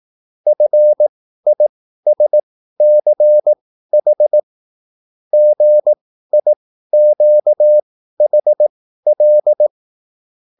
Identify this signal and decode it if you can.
Morse code, heard as a keyed tone, about 18 wpm, spaced as for 13 wpm, 600 Hz. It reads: FISCH GIQHL